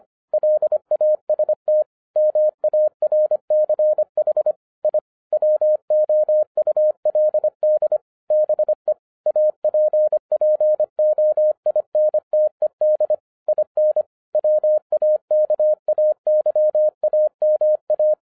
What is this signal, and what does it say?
Morse code, heard as a keyed tone, about 25 wpm, 610 Hz. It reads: LAST MARCH I WOULD BE APPOINTED IN WAKAYAMA
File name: 2011-10-mp3 / n04-JUST MOVED 25wpm